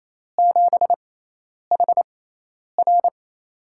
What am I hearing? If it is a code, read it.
Morse code, heard as a keyed tone, about 27 wpm, spaced as for 10 wpm, 700 Hz. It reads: 7HR
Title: USA Call Signs - 4 Characters - 6x Character Spacing - 28wpm